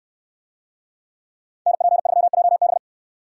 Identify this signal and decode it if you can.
Morse code, heard as a keyed tone, about 68 wpm, 700 Hz. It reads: NP3JB